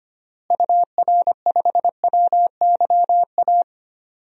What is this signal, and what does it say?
Morse code, heard as a keyed tone, about 25 wpm, 700 Hz. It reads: UR5WYA
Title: DXCC-100 Call Signs - 6 Characters - Repeated 2x - 25wpm